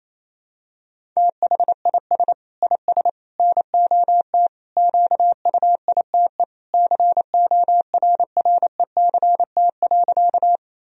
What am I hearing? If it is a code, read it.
Morse code, heard as a keyed tone, about 28 wpm, 700 Hz. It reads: THIS IS NOT QUITE CORRECT.
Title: Sentences from Top 1000 Words - Review Part 2 - 28wpm